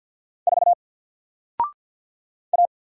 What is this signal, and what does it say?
Morse code, heard as a keyed tone, about 48 wpm, 700 Hz. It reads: 4 A
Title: Single Letter-Number Alt 50wpm